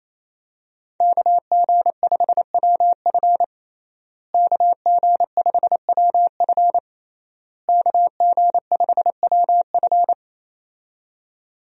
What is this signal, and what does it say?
Morse code, heard as a keyed tone, about 27 wpm, 700 Hz. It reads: KG5WF KG5WF KG5WF
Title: USA Call Signs - 5 Characters - Repeated 3x - 28wpm